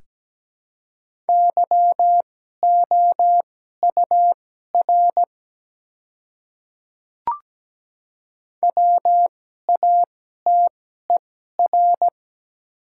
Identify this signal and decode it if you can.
Morse code, heard as a keyed tone, about 17 wpm, spaced as for 12 wpm, 700 Hz. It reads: YOUR WATER
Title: 2x Character Spacing - Top 100 Words - 17wpm